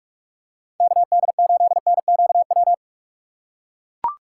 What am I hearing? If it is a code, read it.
Morse code, heard as a keyed tone, about 43 wpm, 700 Hz. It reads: KD8NQW
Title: USA Call Signs - 6 Characters - 4x Character Spacing - 45wpm